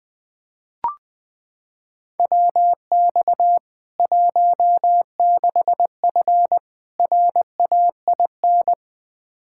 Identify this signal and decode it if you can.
Morse code, heard as a keyed tone, about 20 wpm, 700 Hz. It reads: WX 16F RAIN
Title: K1USN SST QSO Practice 20wpm